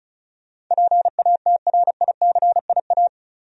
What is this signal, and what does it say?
Morse code, heard as a keyed tone, about 36 wpm, 700 Hz. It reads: PATRICIA
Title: USA Common Names - 35wpm